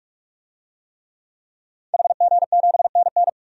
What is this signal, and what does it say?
Morse code, heard as a keyed tone, about 43 wpm, 700 Hz. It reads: HG7NN
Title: DXCC-100 Call Signs - 5 Characters - 45wpm